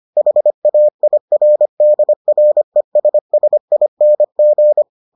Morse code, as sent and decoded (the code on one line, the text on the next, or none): .... .- .. .-. -.. .-. ... ... .. -. --.
HAIRDRESSING